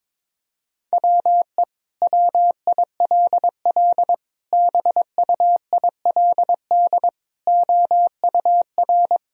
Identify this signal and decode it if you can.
Morse code, heard as a keyed tone, about 22 wpm, 700 Hz. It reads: WE WILL BUILD OUR
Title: Sentences from Top 1000 Words - Review Part 4 - 22wpm